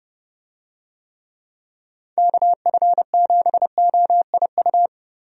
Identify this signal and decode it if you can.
Morse code, heard as a keyed tone, about 30 wpm, 700 Hz. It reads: KF7OIU
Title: USA Call Signs - 6 Characters - 30wpm